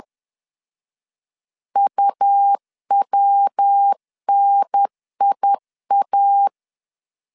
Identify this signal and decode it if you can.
Morse code, heard as a keyed tone, about 10 wpm, 800 Hz. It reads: UWNIA
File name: CW / DET / 1 3 1500C S B 10wpm 3335